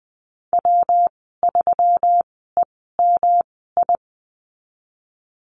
Morse code, heard as a keyed tone, about 20 wpm, spaced as for 15 wpm, 700 Hz. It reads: W3EMI
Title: 128 Lesson 21g - Number 6 - Call Signs 20wpm